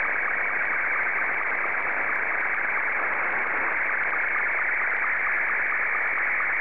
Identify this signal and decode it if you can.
SSTV (slow-scan television), Martin 2.
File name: sstv / M2-VK3EME